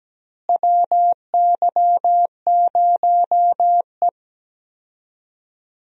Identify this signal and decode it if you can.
Morse code, heard as a keyed tone, about 17 wpm, 700 Hz. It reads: WY0E
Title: USA Call Signs - 4 Characters - Repeated 3x - 17wpm